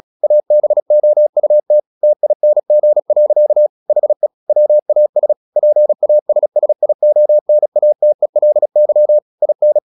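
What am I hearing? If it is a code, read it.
Morse code, heard as a keyed tone, about 36 wpm, 600 Hz. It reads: ABOUT TING. HE WAS PASSIONATELY IN
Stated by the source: Morse practice at 36 wpm, text of Ken Follett's novel Never, from CW med Gnister.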